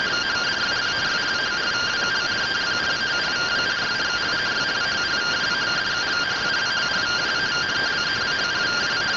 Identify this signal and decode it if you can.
Two-tone teleprinter signal, about 170 Hz shift.